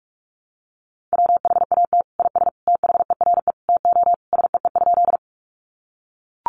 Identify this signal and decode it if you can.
Morse code, heard as a keyed tone, about 44 wpm, 700 Hz. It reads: WHAT IS THERE TO SEE?
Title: Sentences from Top 100 Words 45wpm